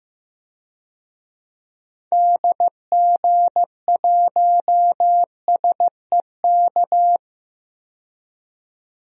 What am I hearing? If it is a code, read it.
Morse code, heard as a keyed tone, about 15 wpm, 700 Hz. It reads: DG1SEK
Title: DXCC-100 Call Signs - 6 Characters - 15wpm